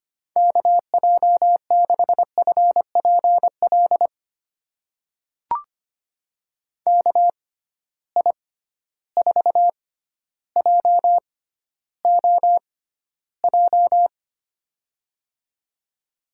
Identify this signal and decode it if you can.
Morse code, heard as a keyed tone, about 25 wpm, spaced as for 13 wpm, 700 Hz. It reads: KJ6FPL KI4JOJ
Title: USA Call Signs - 6 Characters - 6x Character Spacing - 25wpm